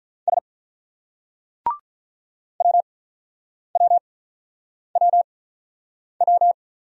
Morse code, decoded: S WWWW

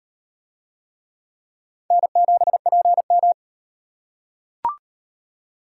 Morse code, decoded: N7PM

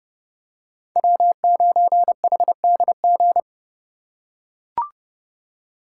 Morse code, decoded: W9HDG

